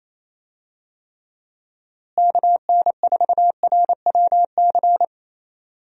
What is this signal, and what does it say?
Morse code, heard as a keyed tone, about 28 wpm, 700 Hz. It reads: KN4RWC
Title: USA Call Signs - 6 Characters - Repeated 2x - 28wpm